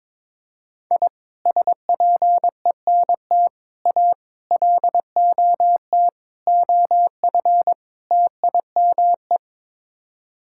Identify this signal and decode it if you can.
Morse code, heard as a keyed tone, about 22 wpm, 700 Hz. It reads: I SPENT A LOT OF TIME
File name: practice / 22wpm / Sets of 6 Words - 22wpm